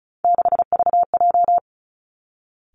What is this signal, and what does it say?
Morse code, heard as a keyed tone, about 36 wpm, 700 Hz. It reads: 6VJ
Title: Three Challenging Letter-Number Combinations 35wpm